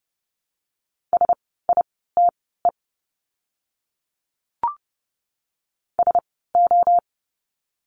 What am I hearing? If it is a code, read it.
Morse code, heard as a keyed tone, about 30 wpm, spaced as for 17 wpm, 700 Hz. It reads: SITE SO
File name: Lesson 17f - Letter u - Words 30wpm